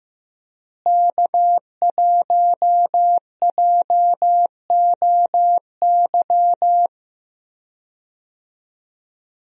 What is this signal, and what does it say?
Morse code, heard as a keyed tone, about 15 wpm, 700 Hz. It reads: K1JOY
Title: USA Call Signs - 5 Characters - Repeated 2x - 15wpm